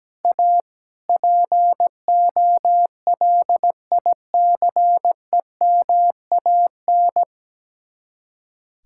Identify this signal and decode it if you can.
Morse code, heard as a keyed tone, about 17 wpm, 700 Hz. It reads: A POLICEMAN
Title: Speed-Racing Common QSO Phrases - 17wpm